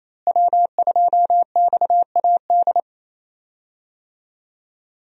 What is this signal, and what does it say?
Morse code, heard as a keyed tone, about 28 wpm, 700 Hz. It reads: W2XAD
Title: USA Call Signs - 5 Characters - Repeated 3x - 28wpm